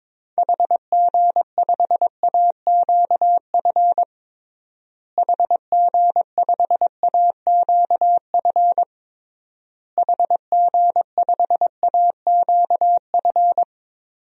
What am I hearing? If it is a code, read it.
Morse code, heard as a keyed tone, about 22 wpm, 700 Hz. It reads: HG5AQF HG5AQF HG5AQF
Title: DXCC-100 Call Signs - 6 Characters - Repeated 3x - 22wpm